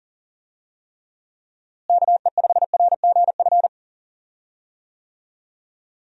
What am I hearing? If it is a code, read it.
Morse code, heard as a keyed tone, about 39 wpm, 700 Hz. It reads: KE5RGF
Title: USA Call Signs - 6 Characters - Repeated 2x - 40wpm